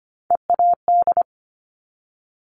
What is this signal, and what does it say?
Morse code, heard as a keyed tone, about 25 wpm, 700 Hz. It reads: EAD
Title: Top Three Letter Combinations in Words 25wpm